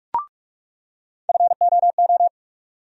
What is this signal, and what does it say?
Morse code, heard as a keyed tone, about 43 wpm, 700 Hz. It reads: FOO
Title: USA Call Sign Postfixes - 3 Letters 45wpm